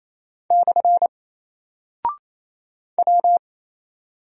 Morse code, heard as a keyed tone, about 27 wpm, 700 Hz. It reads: / W
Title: Single Character (Letter:Number:Punctuation:Prosign) Alt 28wpm